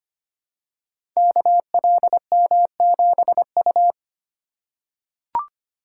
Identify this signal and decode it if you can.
Morse code, heard as a keyed tone, about 25 wpm, 700 Hz. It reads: KLM7U
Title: Code Groups - 5 Letter:Numbers - 25wpm